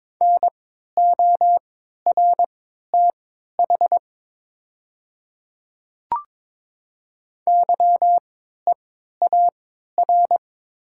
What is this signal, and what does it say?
Morse code, heard as a keyed tone, about 22 wpm, spaced as for 13 wpm, 700 Hz. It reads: NORTH YEAR
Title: 3x Character Spacing - Top 300 Words - 22wpm